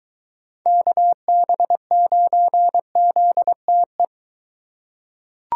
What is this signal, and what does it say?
Morse code, heard as a keyed tone, about 22 wpm, 700 Hz. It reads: KB9ZTE